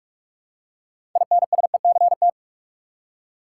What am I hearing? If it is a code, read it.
Morse code, heard as a keyed tone, about 45 wpm, 700 Hz. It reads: INSECT
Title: Six Letter Words - 45wpm